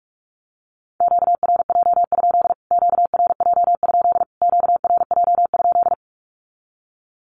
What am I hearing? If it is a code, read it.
Morse code, heard as a keyed tone, about 44 wpm, 700 Hz. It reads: QRJ? QRJ? QRJ?
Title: Q-Codes Repeated 3x 45wpm